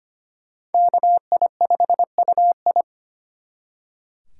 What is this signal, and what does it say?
Morse code, heard as a keyed tone, about 32 wpm, 700 Hz. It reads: KI5UI